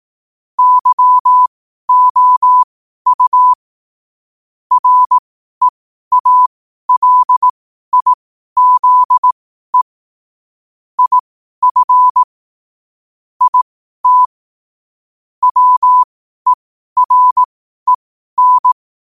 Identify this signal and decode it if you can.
Morse code, heard as a keyed tone, about 18 wpm, spaced as for 12 wpm, 1000 Hz. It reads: YOU REALIZE IF IT WEREN
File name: qotdfiles / 12wpm